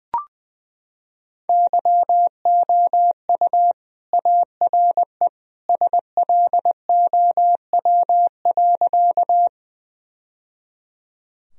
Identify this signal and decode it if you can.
Morse code, heard as a keyed tone, about 20 wpm, 700 Hz. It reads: YOU ARE SLOW.
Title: Sentences from Top 1000 Words - Review Part 2 - 20wpm